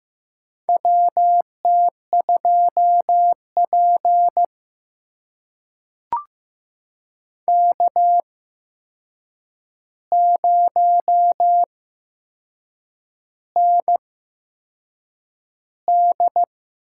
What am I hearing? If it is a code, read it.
Morse code, heard as a keyed tone, about 15 wpm, spaced as for 6 wpm, 700 Hz. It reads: WT2P K0ND
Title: USA Call Signs - 4 Characters - 8x Character Spacing - 15wpm